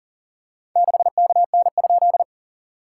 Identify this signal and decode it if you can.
Morse code, heard as a keyed tone, about 44 wpm, 700 Hz. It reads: 6KN?
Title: SOTA QSO Practice 40wpm